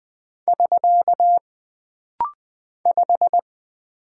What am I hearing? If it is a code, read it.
Morse code, heard as a keyed tone, about 20 wpm, 700 Hz. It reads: <SK> 5
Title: Single Character (Letter:Number:Punctuation:Prosign) - Rapid-Fire - 20wpm